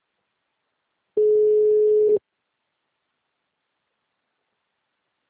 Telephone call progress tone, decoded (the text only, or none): ringback tone